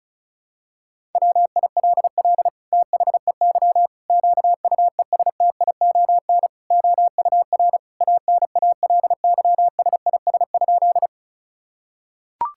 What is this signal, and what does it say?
Morse code, heard as a keyed tone, about 34 wpm, 700 Hz. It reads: WILL THEY QUESTION OUR ANALYSIS?
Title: Sentences from Top 1000 Words - Review Part 5 - 35wpm